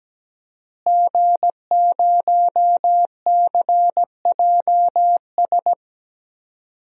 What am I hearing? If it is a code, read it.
Morse code, heard as a keyed tone, about 17 wpm, 700 Hz. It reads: G0CJS
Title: DXCC-100 Call Signs - 5 Characters - Repeated 2x - 17wpm